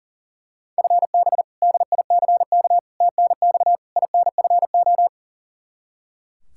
Morse code, heard as a keyed tone, about 37 wpm, 700 Hz. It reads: FB DICK TNX INFO
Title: Practice QSO - Exchange 3 - More Info - 40wpm